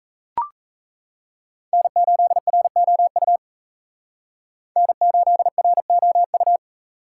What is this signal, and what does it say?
Morse code, heard as a keyed tone, about 35 wpm, 700 Hz. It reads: N8ROU N8ROU